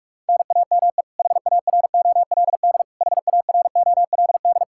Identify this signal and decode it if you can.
Morse code, heard as a keyed tone, about 46 wpm, 700 Hz. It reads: NAME HAROLD HAROLD
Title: Practice QSO - Exchange 1 - Beginning - 45wpm